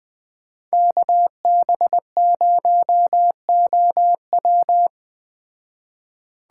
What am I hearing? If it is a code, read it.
Morse code, heard as a keyed tone, about 20 wpm, 700 Hz. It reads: KB0OW
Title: Speed Racing - USA 2x2 Call Signs 17wpm